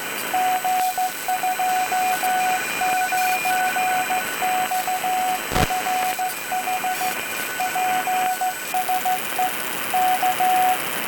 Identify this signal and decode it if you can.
Morse code, heard as a keyed tone, about 14 wpm, 740 Hz, with radio noise. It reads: G29XFH PSE K